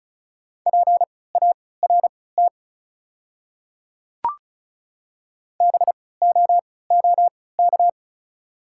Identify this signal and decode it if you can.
Morse code, heard as a keyed tone, about 34 wpm, spaced as for 20 wpm, 700 Hz. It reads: PART BOOK
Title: 3x Character Spacing - Top 300 Words - 35wpm